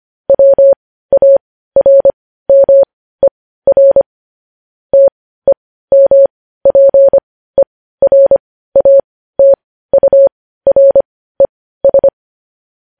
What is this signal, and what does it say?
Morse code, heard as a keyed tone, about 25 wpm, spaced as for 15 wpm, 550 Hz. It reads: WARMER TEMPERATURES
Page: News Headlines in Morse Code at 15 WPM